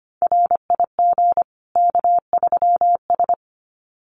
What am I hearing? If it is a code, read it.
Morse code, heard as a keyed tone, about 25 wpm, 700 Hz. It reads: RIG K3S
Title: Common QSO Phrases 25wpm